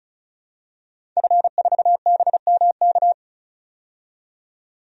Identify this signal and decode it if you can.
Morse code, heard as a keyed tone, about 34 wpm, 700 Hz. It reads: F4BMK
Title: DXCC-100 Call Signs - 5 Characters - 35wpm